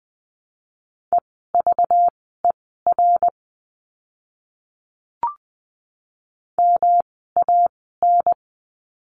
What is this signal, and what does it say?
Morse code, heard as a keyed tone, about 20 wpm, spaced as for 14 wpm, 700 Hz. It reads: EVER MAN
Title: 192 Lesson 31f - Letter v - Words 20wpm